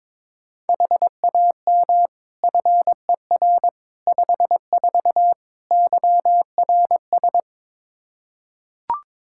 Morse code, heard as a keyed tone, about 21 wpm, 700 Hz. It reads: HAM FER 54 YRS
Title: K1USN SST QSO Practice 22wpm